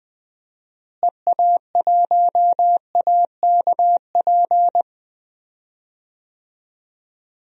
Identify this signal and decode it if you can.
Morse code, heard as a keyed tone, about 21 wpm, 700 Hz. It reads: EA1AKP
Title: DXCC-100 Call Signs - 6 Characters - Repeated 3x - 20wpm